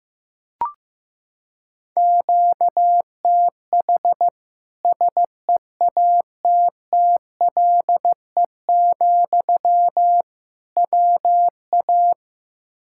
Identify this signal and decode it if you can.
Morse code, heard as a keyed tone, about 15 wpm, 700 Hz. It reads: QTH SEATTLE, WA
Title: Practice QSO - Exchange 1 - Beginning - 15wpm